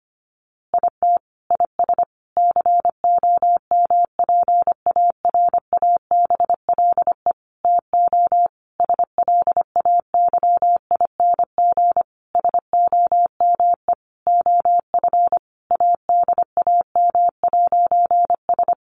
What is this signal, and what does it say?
Morse code, heard as a keyed tone, about 25 wpm, 700 Hz. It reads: IT IS COMPARABLE TO SLAYING SOME OF ADAM'S
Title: A Sound Of Thunder 25wpm